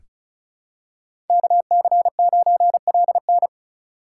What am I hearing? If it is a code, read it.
Morse code, heard as a keyed tone, about 36 wpm, 700 Hz. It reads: KC9LN
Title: USA Call Signs - 5 Characters - 4x Character Spacing - 35wpm